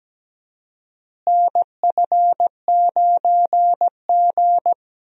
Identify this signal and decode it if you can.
Morse code, heard as a keyed tone, about 17 wpm, 700 Hz. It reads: NF9G
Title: USA Call Signs - 4 Characters - Repeated 3x - 17wpm